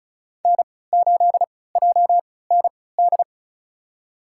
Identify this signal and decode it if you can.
Morse code, heard as a keyed tone, about 34 wpm, spaced as for 20 wpm, 700 Hz. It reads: N8JND
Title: USA Call Signs - 5 Characters - 3x Character Spacing - 35wpm